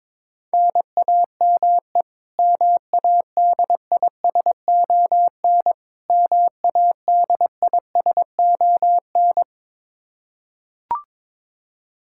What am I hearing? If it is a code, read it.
Morse code, heard as a keyed tone, about 22 wpm, 700 Hz. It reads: NAME MADISON MADISON
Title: K1USN SST QSO Practice 22wpm